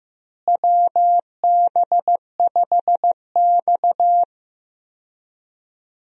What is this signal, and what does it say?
Morse code, heard as a keyed tone, about 15 wpm, 700 Hz. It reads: WB5X
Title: USA Call Signs - 4 Characters - Repeated 3x - 15wpm